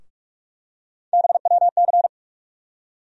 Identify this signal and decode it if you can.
Morse code, heard as a keyed tone, about 46 wpm, 700 Hz. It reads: BWC
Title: USA Call Sign Postfixes - 3 Letters 45wpm